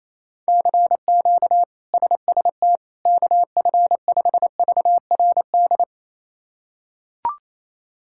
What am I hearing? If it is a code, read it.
Morse code, heard as a keyed tone, about 28 wpm, 700 Hz. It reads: CQ SST KF5VRD
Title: K1USN SST QSO Practice - 28wpm